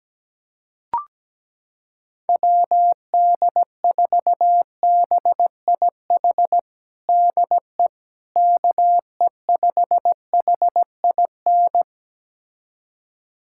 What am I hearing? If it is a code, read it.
Morse code, heard as a keyed tone, about 17 wpm, 700 Hz. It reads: WD4BIH DE KE5HIN
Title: USA Callsign DE Callsign 17wpm